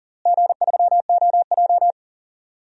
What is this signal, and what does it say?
Morse code, heard as a keyed tone, about 40 wpm, 700 Hz. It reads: G3OJ
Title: DXCC-100 Call Signs - 4 Characters - Repeated 2x - 40wpm